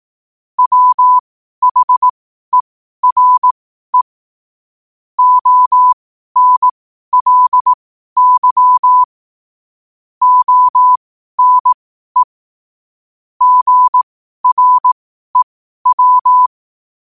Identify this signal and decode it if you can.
Morse code, heard as a keyed tone, about 18 wpm, spaced as for 12 wpm, 1000 Hz. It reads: WHERE ONLY ONE GREW
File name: qotdfiles / 12wpm